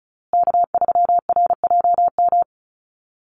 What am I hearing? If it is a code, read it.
Morse code, heard as a keyed tone, about 34 wpm, 700 Hz. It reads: K3RJM
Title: USA Call Signs Repeated 2x 35wpm